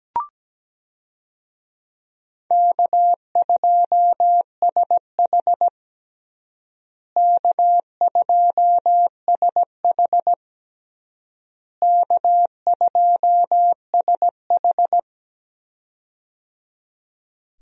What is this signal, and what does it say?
Morse code, heard as a keyed tone, about 17 wpm, 700 Hz. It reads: K2SH K2SH K2SH
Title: USA Call Signs - 4 Characters - Repeated 3x - 17wpm